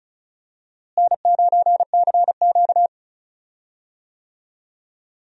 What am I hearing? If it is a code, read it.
Morse code, heard as a keyed tone, about 36 wpm, 700 Hz. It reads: N9CQ
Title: USA Call Signs - 4 Characters - Repeated 2x - 35wpm